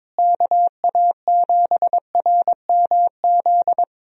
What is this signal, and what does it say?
Morse code, heard as a keyed tone, about 22 wpm, 700 Hz. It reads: KA7RMZ
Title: USA Call Signs - 6 Characters - Repeated 2x - 22wpm